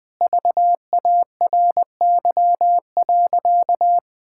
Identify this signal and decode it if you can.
Morse code, heard as a keyed tone, about 20 wpm, 700 Hz. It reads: VARY.
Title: Sentences with 7-Word Phrases - Spoken Beginning - 20wpm